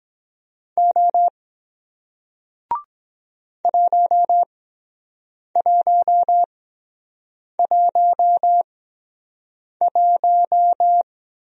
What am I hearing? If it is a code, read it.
Morse code, heard as a keyed tone, about 23 wpm, spaced as for 7 wpm, 700 Hz. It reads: O 1111